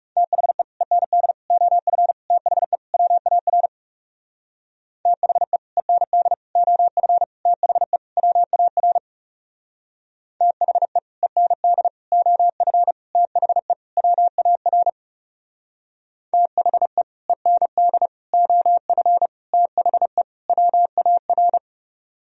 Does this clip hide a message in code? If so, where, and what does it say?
Yes: Morse code from 0.2 to 21.6 s, heard as a keyed tone, about 38 wpm, 700 Hz. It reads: THE END OF THE WAR THE END OF THE WAR THE END OF THE WAR THE END OF THE WAR